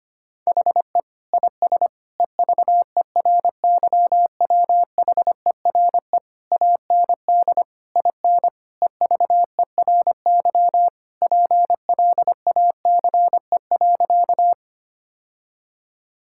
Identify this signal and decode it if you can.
Morse code, heard as a keyed tone, about 25 wpm, 700 Hz. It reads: HE IS EVERYWHERE AND IN EVERY PLACE.
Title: Sentences from Top 1600 Words 25wpm